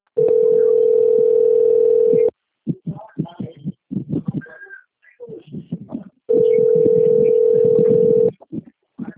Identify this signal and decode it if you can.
Ringback tone (North America), 440 + 480 Hz.